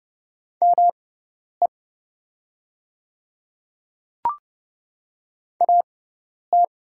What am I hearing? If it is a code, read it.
Morse code, heard as a keyed tone, about 30 wpm, spaced as for 10 wpm, 690 Hz. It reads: ME AT
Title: 6x Character Spacing - Top 200 Words - 30wpm